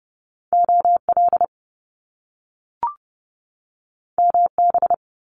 Morse code, heard as a keyed tone, about 30 wpm, 700 Hz. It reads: OL MB
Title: Top Two Letter Combinations in Words 30wpm